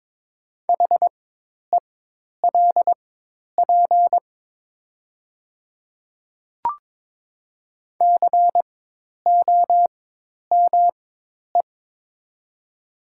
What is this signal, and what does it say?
Morse code, heard as a keyed tone, about 22 wpm, spaced as for 10 wpm, 700 Hz. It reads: HELP COME
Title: 4x Character Spacing - Top 200 Words - 22wpm